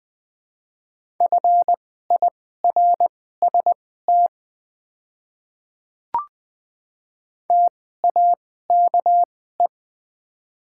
Morse code, heard as a keyed tone, about 21 wpm, spaced as for 15 wpm, 700 Hz. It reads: FIRST TAKE
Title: 2x Character Spacing - Top 300 Words - 20wpm